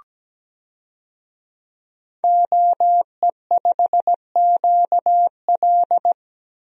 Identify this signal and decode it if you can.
Morse code, heard as a keyed tone, about 17 wpm, 700 Hz. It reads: OE5QL
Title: DXCC-100 Call Signs - 5 Characters - Repeated 2x - 17wpm